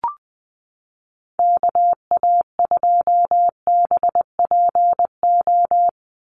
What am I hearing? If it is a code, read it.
Morse code, heard as a keyed tone, about 20 wpm, 700 Hz. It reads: KA2BPO
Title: USA Call Signs 20wpm